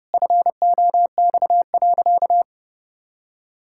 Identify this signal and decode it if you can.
Morse code, heard as a keyed tone, about 30 wpm, 700 Hz. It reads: FOX.
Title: Pangram Challenge v2 - 30wpm